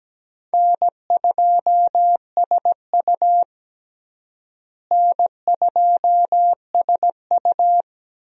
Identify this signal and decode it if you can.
Morse code, heard as a keyed tone, about 17 wpm, 700 Hz. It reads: N2SU N2SU
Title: USA Call Signs - 4 Characters - Repeated 3x - 17wpm